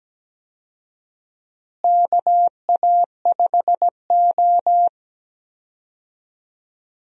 Morse code, heard as a keyed tone, about 17 wpm, 700 Hz. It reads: KA5O
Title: USA Call Signs - 4 Characters - Repeated 2x - 17wpm